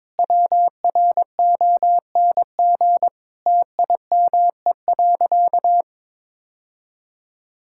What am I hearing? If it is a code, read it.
Morse code, heard as a keyed tone, about 22 wpm, 700 Hz. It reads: WRONG TIME.
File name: practice / 22wpm / Sentences with 8-Word Phrases - 22wpm